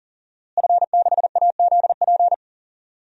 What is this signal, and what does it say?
Morse code, heard as a keyed tone, about 40 wpm, 700 Hz. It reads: F6AZP
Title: DXCC-100 Call Signs - 5 Characters - Repeated 2x - 40wpm